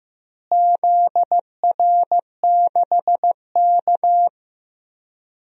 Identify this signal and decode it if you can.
Morse code, heard as a keyed tone, about 15 wpm, 700 Hz. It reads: ZR6K
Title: DXCC-100 Call Signs - 4 Characters - Repeated 3x - 15wpm